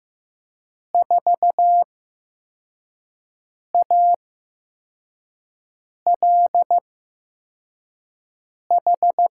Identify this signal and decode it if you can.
Morse code, heard as a keyed tone, about 15 wpm, spaced as for 4 wpm, 700 Hz. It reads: 4ALH
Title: USA Call Signs - 6 Characters - 8x Character Spacing - 15wpm